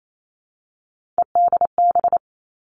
Morse code, heard as a keyed tone, about 28 wpm, 700 Hz. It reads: EDB
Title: Three Challenging Letter-Number Combinations 28wpm